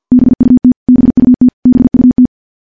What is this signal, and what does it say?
Morse code, heard as a keyed tone, about 50 wpm, 250 Hz. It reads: BUT BUT BUT
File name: morse / morse-training-50WPM